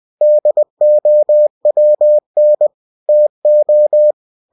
Morse code, heard as a keyed tone, about 20 wpm, 600 Hz. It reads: DOWN TO
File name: cw-books / winnie-the-pooh / winnie-20wpm-page-0007